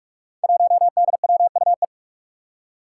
Morse code, heard as a keyed tone, about 45 wpm, 700 Hz. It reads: 1DWUE
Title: Five Challenging Letter-Number Combinations 45wpm